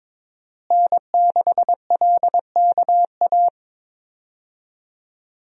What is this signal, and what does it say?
Morse code, heard as a keyed tone, about 22 wpm, 700 Hz. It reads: N6LKA
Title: USA Call Signs - 5 Characters - Repeated 3x - 22wpm